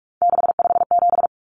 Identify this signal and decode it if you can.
Morse code, heard as a keyed tone, about 44 wpm, 700 Hz. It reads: 657
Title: Common Cut Numbers - 3-Digit 45wpm